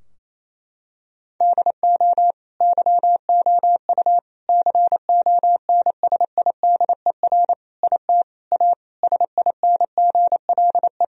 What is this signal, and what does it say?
Morse code, heard as a keyed tone, about 28 wpm, 700 Hz. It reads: DO YOU CONSIDER IT A SINGLE
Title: Sentences from Top 1000 Words - Review Part 4 - 28wpm